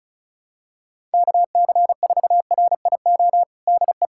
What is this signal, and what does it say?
Morse code, heard as a keyed tone, about 34 wpm, 700 Hz. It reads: KC4RIO DE
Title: USA Callsign DE Callsign 35wpm